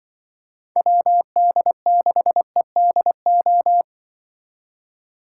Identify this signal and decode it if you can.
Morse code, heard as a keyed tone, about 24 wpm, 700 Hz. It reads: WD6EDO
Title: Speed Racing - USA 2x3 Call Signs 20wpm